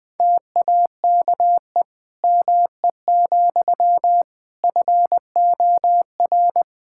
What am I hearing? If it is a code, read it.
Morse code, heard as a keyed tone, about 20 wpm, 700 Hz. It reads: TAKE ME, FOR
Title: Sentences from Top 1000 Words - Review Part 5 - 20wpm